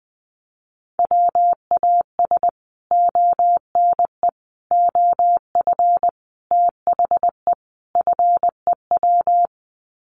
Morse code, heard as a keyed tone, about 20 wpm, 700 Hz. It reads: WAS ONE OF THE FEW
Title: Sets of 5 Words Encore 20wpm